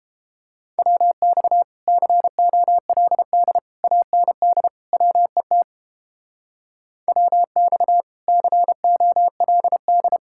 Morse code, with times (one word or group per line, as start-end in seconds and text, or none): 0.8-1.6 WX
1.9-3.6 COLD
3.8-4.7 AND
4.9-5.6 WET
7.1-8.0 WX
8.3-10.2 COLD